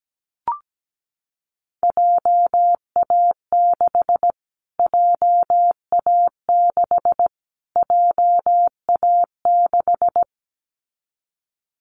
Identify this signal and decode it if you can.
Morse code, heard as a keyed tone, about 17 wpm, 700 Hz. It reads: JA6 JA6 JA6
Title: Callsign Prefixes for Top 25 DXCC - Repeated 3x - 17wpm